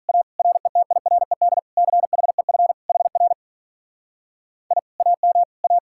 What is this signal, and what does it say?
Morse code, heard as a keyed tone, about 44 wpm, 700 Hz. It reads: A RETIRED CHEF HR I AM A